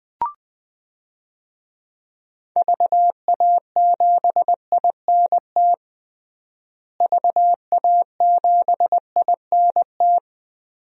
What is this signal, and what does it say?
Morse code, heard as a keyed tone, about 20 wpm, 700 Hz. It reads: VA7INT VA7INT
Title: DXCC-100 Call Signs - 6 Characters - Repeated 3x - 20wpm